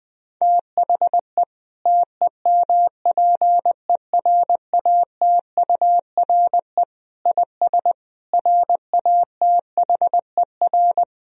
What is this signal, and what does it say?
Morse code, heard as a keyed tone, about 20 wpm, 700 Hz. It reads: THE TEMPERATURE IS RATHER
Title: Sentences from Top 1000 Words - Review Part 4 - 20wpm